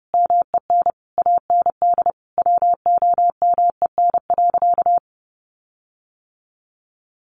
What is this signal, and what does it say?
Morse code, heard as a keyed tone, about 30 wpm, 700 Hz. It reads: MEN AND WOMEN.
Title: Sentences from Top 1300 Words 30wpm